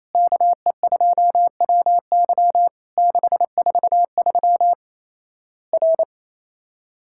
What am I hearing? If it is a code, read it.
Morse code, heard as a keyed tone, about 27 wpm, 700 Hz. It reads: KE2WY 643 R
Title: Worked All Europe DX Contest - QTC Exchange - 28wpm